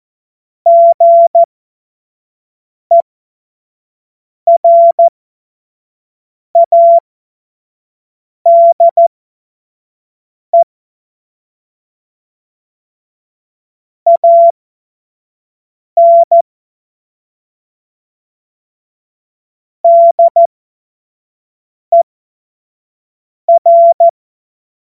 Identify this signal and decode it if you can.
Morse code, heard as a keyed tone, about 14 wpm, spaced as for 5 wpm, 680 Hz. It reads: GERADE AN DER
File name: Morsekurs / Witze / Witz 10. August 2021, 25 F70